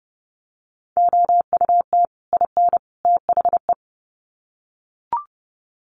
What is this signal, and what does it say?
Morse code, heard as a keyed tone, about 30 wpm, 700 Hz. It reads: OUT IN THE
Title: Sets of 3 Words 30wpm